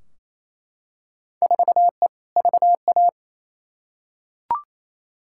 Morse code, heard as a keyed tone, about 28 wpm, 700 Hz. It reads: 4E VA
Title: Field Day Exchanges Encore - 28wpm